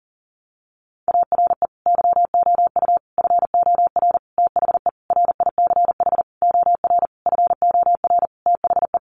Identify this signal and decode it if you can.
Morse code, heard as a keyed tone, about 40 wpm, 700 Hz. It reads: ARE YOU FOR THE RICH OR FOR THE
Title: Sentences from Top 1100 Words 40wpm